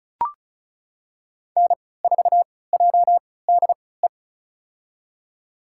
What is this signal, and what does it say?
Morse code, heard as a keyed tone, about 36 wpm, spaced as for 20 wpm, 700 Hz. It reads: N4JDE
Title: USA Call Signs - 5 Characters - 3x Character Spacing - 35wpm